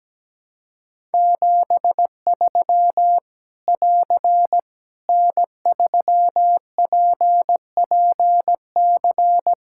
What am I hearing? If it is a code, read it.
Morse code, heard as a keyed tone, about 17 wpm, 700 Hz. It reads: 73 <AR> N3PPC
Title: K1USN SST QSO Practice 17wpm